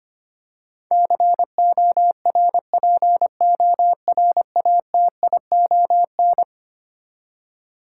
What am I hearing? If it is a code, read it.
Morse code, heard as a keyed tone, about 25 wpm, 700 Hz. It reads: CORPORATION
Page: morse-code-ninja-practice-25wpm